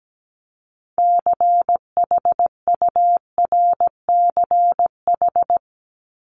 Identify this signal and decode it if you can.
Morse code, heard as a keyed tone, about 17 wpm, 700 Hz. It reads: CHURCH
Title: Top 1751-2000 Words 17wpm